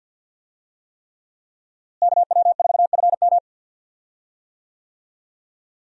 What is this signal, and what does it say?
Morse code, heard as a keyed tone, about 53 wpm, 700 Hz. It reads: KW4FM
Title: USA Call Signs - 5 Characters - Repeated 3x - 50wpm